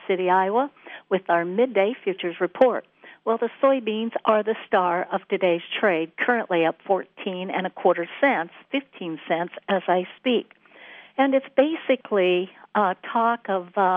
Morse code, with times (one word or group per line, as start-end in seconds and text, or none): none